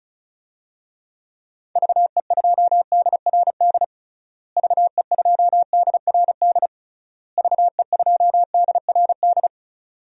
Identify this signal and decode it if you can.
Morse code, heard as a keyed tone, about 34 wpm, 700 Hz. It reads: VE2DRD VE2DRD VE2DRD
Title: DXCC-100 Call Signs - 6 Characters - Repeated 3x - 35wpm